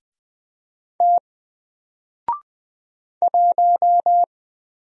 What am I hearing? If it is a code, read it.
Morse code, heard as a keyed tone, about 20 wpm, 700 Hz. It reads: T 1
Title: Single Letter-Number 20wpm